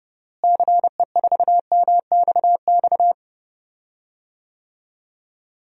Morse code, heard as a keyed tone, about 32 wpm, 700 Hz. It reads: CE4MXX
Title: DXCC-100 Call Signs - 6 Characters - Repeated 2x - 30wpm